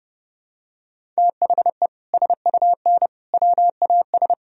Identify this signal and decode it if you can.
Morse code, heard as a keyed tone, about 32 wpm, 700 Hz. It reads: THE SUN WAS
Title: Sentences from Top 1000 Words - Review Part 1 - 30wpm